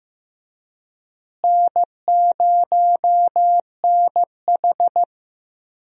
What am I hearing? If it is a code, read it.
Morse code, heard as a keyed tone, about 15 wpm, 700 Hz. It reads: N0NH
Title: USA Call Signs - 4 Characters - Repeated 2x - 15wpm